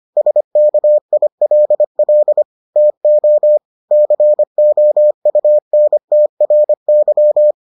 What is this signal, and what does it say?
Morse code, heard as a keyed tone, about 25 wpm, 600 Hz. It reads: SKILL TO COUNTRY